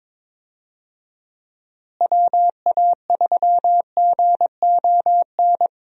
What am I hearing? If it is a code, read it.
Morse code, heard as a keyed tone, about 22 wpm, 700 Hz. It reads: WA3GON
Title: USA Call Signs - 6 Characters - Repeated 2x - 22wpm